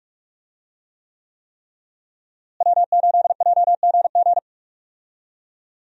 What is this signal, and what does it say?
Morse code, heard as a keyed tone, about 41 wpm, 700 Hz. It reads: W8JGG
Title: USA Call Signs - 5 Characters - Repeated 3x - 45wpm